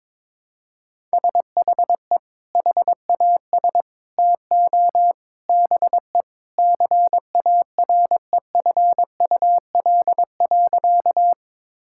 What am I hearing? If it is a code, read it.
Morse code, heard as a keyed tone, about 22 wpm, 700 Hz. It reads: SHE HAS TO BE CAREFUL.
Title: Sentences from Top 1600 Words 22wpm